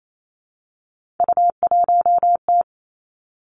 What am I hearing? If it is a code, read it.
Morse code, heard as a keyed tone, about 28 wpm, 690 Hz. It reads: U1T
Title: Three Challenging Letter-Number Combinations 28wpm